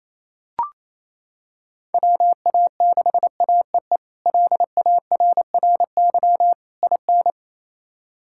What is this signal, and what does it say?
Morse code, heard as a keyed tone, about 28 wpm, 700 Hz. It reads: WA6AEE LARRY IN
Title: K1USN SST QSO Practice - 28wpm